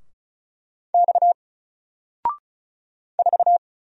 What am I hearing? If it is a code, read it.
Morse code, heard as a keyed tone, about 36 wpm, 700 Hz. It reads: X 4
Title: Single Letter-Number Alt 35wpm